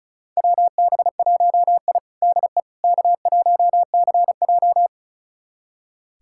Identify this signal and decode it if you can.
Morse code, heard as a keyed tone, about 34 wpm, 700 Hz. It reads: WB1I DE K1CJ
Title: USA Callsign DE Callsign - 4 Characters - 35wpm